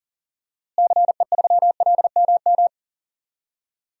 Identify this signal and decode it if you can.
Morse code, heard as a keyed tone, about 40 wpm, 700 Hz. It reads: CE3LMM
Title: DXCC-100 Call Signs - 6 Characters - Repeated 3x - 40wpm